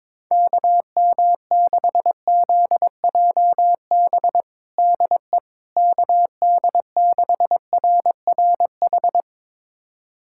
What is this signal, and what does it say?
Morse code, heard as a keyed tone, about 21 wpm, 700 Hz. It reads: KM6ZJB DE KD6RRH
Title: USA Callsign DE Callsign 22wpm